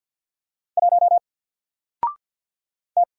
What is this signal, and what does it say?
Morse code, heard as a keyed tone, about 50 wpm, 700 Hz. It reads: 1 T
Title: Single Letter-Number 50wpm